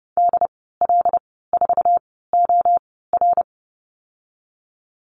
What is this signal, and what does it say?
Morse code, heard as a keyed tone, about 30 wpm, spaced as for 17 wpm, 690 Hz. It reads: DL4OR
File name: Lesson 15g - Number 5 - Call Signs 30wpm